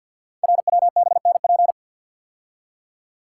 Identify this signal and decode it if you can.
Morse code, heard as a keyed tone, about 51 wpm, 700 Hz. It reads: RWBNP
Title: Code Groups - 5 Letters - 50wpm